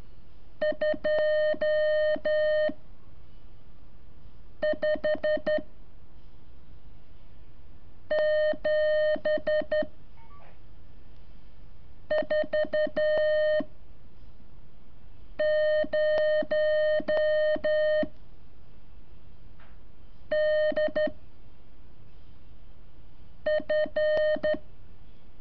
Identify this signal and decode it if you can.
Morse code, heard as a keyed tone, about 12 wpm, 610 Hz. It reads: 2 5 7 4 0 D F
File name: curso / ejercicio 24